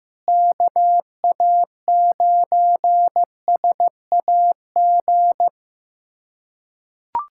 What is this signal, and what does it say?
Morse code, heard as a keyed tone, about 15 wpm, 700 Hz. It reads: KA9SAG
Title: USA Callsign DE Callsign - 6 Characters - 15wpm